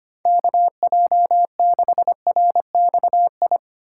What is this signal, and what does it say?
Morse code, heard as a keyed tone, about 31 wpm, 700 Hz. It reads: KJ6RXI
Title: Speed Racing - USA 2x3 Call Signs 25wpm